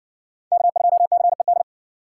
Morse code, heard as a keyed tone, about 54 wpm, 700 Hz. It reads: K2CED